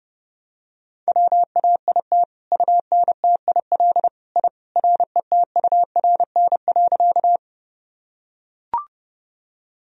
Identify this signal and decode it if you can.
Morse code, heard as a keyed tone, about 30 wpm, 700 Hz. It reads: WAIT UNTIL I RETURN.
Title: Sentences from Top 1000 Words - Review Part 1 - 30wpm